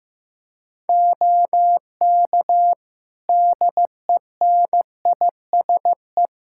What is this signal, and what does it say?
Morse code, heard as a keyed tone, about 16 wpm, 700 Hz. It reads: OK DENISE
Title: K1USN SST QSO Practice 15wpm